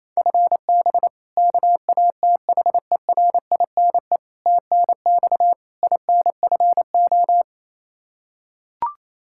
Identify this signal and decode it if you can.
Morse code, heard as a keyed tone, about 28 wpm, 700 Hz. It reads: FB KATHERINE TNX INFO
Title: Practice QSO - Exchange 3 - More Info - 28wpm